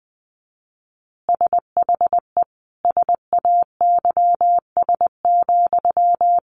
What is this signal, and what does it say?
Morse code, heard as a keyed tone, about 20 wpm, 700 Hz. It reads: SHE SAYS,
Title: Easy Phrases 20wpm